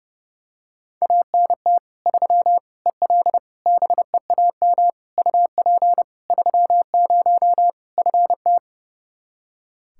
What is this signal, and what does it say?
Morse code, heard as a keyed tone, about 29 wpm, 700 Hz. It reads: ANT 3 EL BEAM UP 30 FT
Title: Practice QSO - Exchange 2 - Info - 30wpm